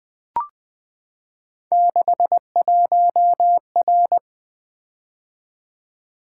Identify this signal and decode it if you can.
Morse code, heard as a keyed tone, about 20 wpm, 700 Hz. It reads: 61R
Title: Three Letter-Number Combinations - 20wpm